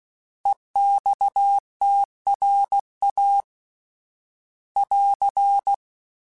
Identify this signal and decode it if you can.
Morse code, heard as a keyed tone, about 16 wpm, 800 Hz. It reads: EXTRA <AR>